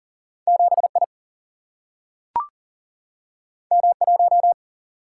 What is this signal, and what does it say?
Morse code, heard as a keyed tone, about 41 wpm, 700 Hz. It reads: 7I M1
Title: Single-Letter Single-Number 40wpm